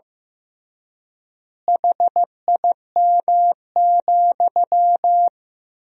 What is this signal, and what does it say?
Morse code, heard as a keyed tone, about 15 wpm, 700 Hz. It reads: HIM,
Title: 3x Word-Spacing - Sentences from Top 400 Words - 15wpm